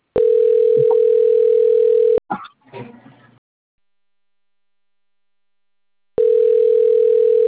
Ringback tone (North America), 440 + 480 Hz.